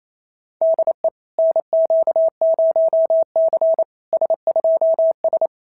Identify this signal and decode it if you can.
Morse code, heard as a keyed tone, about 31 wpm, 650 Hz. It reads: DE NQ0C S2S
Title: SOTA QSO Practice 28wpm